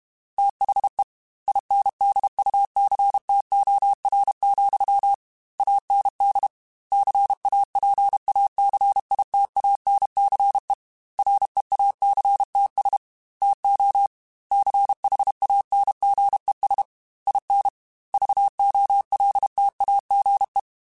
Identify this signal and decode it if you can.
Morse code, heard as a keyed tone, about 32 wpm, 800 Hz. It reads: THE INDUCTOR, AND CAPACITANCE REACTS TO CHANGES IN VOLTAGE
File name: Technical Subjects / Technical Subjects-mp3s / Reactance @30